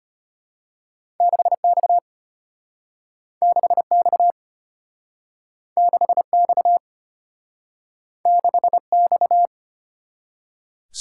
Morse code, heard as a keyed tone, about 32 wpm, 700 Hz. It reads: 6X 6X 6X 6X